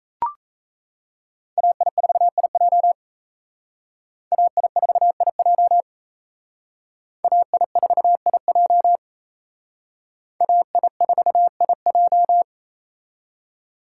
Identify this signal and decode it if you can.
Morse code, heard as a keyed tone, about 35 wpm, 700 Hz. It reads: AI4IJ AI4IJ AI4IJ AI4IJ
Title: Speed Racing - USA 2x2 Call Signs 28wpm